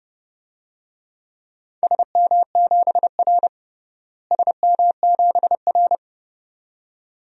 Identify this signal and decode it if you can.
Morse code, heard as a keyed tone, about 30 wpm, 700 Hz. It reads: SM7R SM7R
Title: DXCC-100 Call Signs - 4 Characters - Repeated 2x - 30wpm